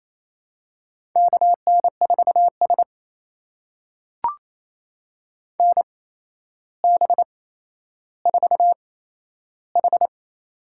Morse code, heard as a keyed tone, about 28 wpm, spaced as for 12 wpm, 700 Hz. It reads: KN4S NB4H
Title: USA Call Signs - 4 Characters - 8x Character Spacing - 28wpm